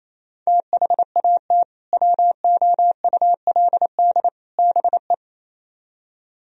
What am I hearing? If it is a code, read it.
Morse code, heard as a keyed tone, about 35 wpm, 700 Hz. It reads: THAT WOULD BE